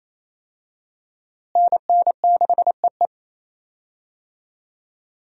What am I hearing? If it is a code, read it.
Morse code, heard as a keyed tone, about 28 wpm, 700 Hz. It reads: NN6EE
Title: USA Call Signs - 5 Characters - Repeated 2x - 28wpm